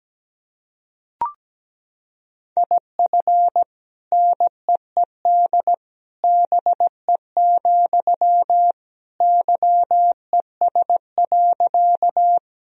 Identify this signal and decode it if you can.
Morse code, heard as a keyed tone, about 17 wpm, 700 Hz. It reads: IF NEED BE, YES.
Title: Sentences from Top 1000 Words - Review Part 1 - 17wpm